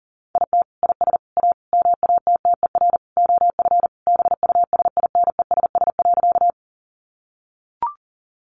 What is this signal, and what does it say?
Morse code, heard as a keyed tone, about 40 wpm, 700 Hz. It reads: IT IS A MATTER OF BUSINESS.
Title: Sentences from Top 700 Words 40wpm